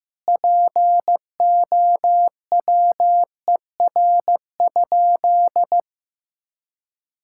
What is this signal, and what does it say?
Morse code, heard as a keyed tone, about 16 wpm, 700 Hz. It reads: POWER?
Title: Sentences from Top 1000 Words - Review Part 6 - 15wpm